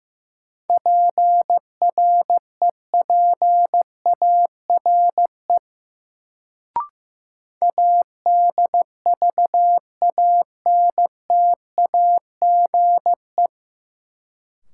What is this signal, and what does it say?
Morse code, heard as a keyed tone, about 15 wpm, 700 Hz. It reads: PREPARE ADVANTAGE
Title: Sentences from Top 900 Words - New Word Drill - 15wpm